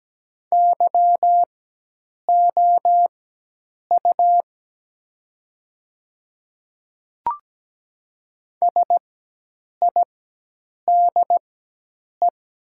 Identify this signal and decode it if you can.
Morse code, heard as a keyed tone, about 17 wpm, spaced as for 8 wpm, 700 Hz. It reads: YOU SIDE
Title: 4x Character Spacing - Top 100 Words - 17wpm